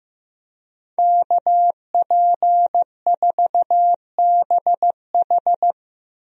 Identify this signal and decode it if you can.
Morse code, heard as a keyed tone, about 15 wpm, 700 Hz. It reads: KP4BH
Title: DXCC-100 Call Signs - 5 Characters - 15wpm